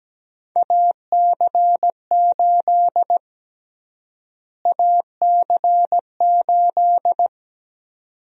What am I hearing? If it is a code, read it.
Morse code, heard as a keyed tone, about 17 wpm, 700 Hz. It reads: AC8 AC8
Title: USA Call Sign Prefixes - Repeated 2x - 17wpm